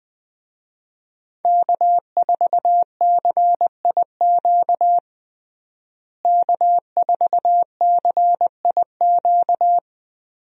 Morse code, heard as a keyed tone, about 20 wpm, 700 Hz. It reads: K4CIQ K4CIQ
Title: USA Call Signs - 5 Characters - Repeated 3x - 20wpm